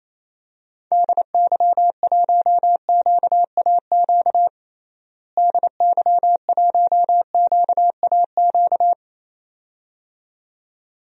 Morse code, heard as a keyed tone, about 28 wpm, 700 Hz. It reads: DY1QAQ DY1QAQ
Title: DXCC-100 Call Signs - 6 Characters - Repeated 2x - 28wpm